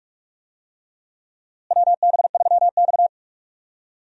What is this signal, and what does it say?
Morse code, heard as a keyed tone, about 46 wpm, 700 Hz. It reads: WB3X
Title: USA Call Signs - 4 Characters - Repeated 3x - 45wpm